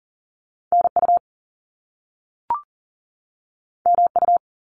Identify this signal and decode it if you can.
Morse code, heard as a keyed tone, about 42 wpm, 700 Hz. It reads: NU MU
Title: Top Two Letter Combinations in Words 40wpm